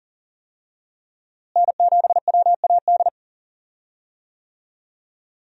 Morse code, heard as a keyed tone, about 40 wpm, 700 Hz. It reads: N7WAD
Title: USA Call Signs - 5 Characters - Repeated 2x - 40wpm